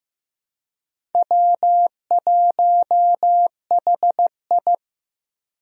Morse code, heard as a keyed tone, about 15 wpm, 700 Hz. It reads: W1HI
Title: USA Call Signs - 4 Characters - Repeated 3x - 15wpm